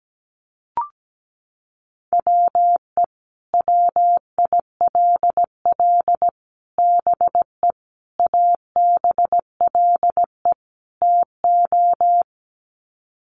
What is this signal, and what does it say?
Morse code, heard as a keyed tone, about 17 wpm, 700 Hz. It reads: WE WILL BE ABLE TO
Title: Sets of 5 Words Encore 17wpm